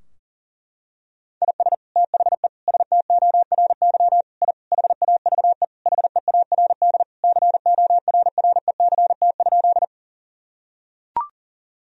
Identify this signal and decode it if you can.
Morse code, heard as a keyed tone, about 40 wpm, 700 Hz. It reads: IS THE STORY I HAVE HEARD CORRECT?
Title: Sentences from Top 1000 Words - Review Part 2 - 40wpm